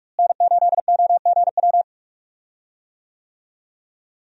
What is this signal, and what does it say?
Morse code, heard as a keyed tone, about 44 wpm, 700 Hz. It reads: N8OGW
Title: USA Call Signs - 5 Characters - Repeated 2x - 45wpm